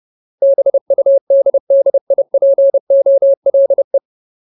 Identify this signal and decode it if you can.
Morse code, heard as a keyed tone, about 30 wpm, 550 Hz. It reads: BUDDIPOLE
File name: Ultimate Word List / 30WPM Ultimate